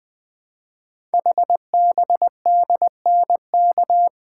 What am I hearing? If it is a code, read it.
Morse code, heard as a keyed tone, about 20 wpm, 700 Hz. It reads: HBDNK
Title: Code Groups - 5 Letters - 20wpm